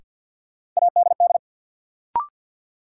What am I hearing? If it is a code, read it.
Morse code, heard as a keyed tone, about 52 wpm, 700 Hz. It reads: ADD
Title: Sentences from Top 1000 Words - Review Part 1 - New Word Drill - 50wpm